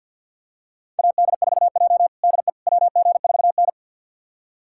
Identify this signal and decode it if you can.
Morse code, heard as a keyed tone, about 49 wpm, 700 Hz. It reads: AD4J DE WG4N
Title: USA Callsign DE Callsign - 4 Characters - 50wpm